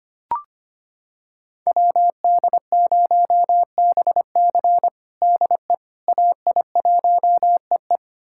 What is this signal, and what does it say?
Morse code, heard as a keyed tone, about 25 wpm, 700 Hz. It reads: WD0BC DE AI1EE
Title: USA Callsign DE Callsign - 5 Characters - 25wpm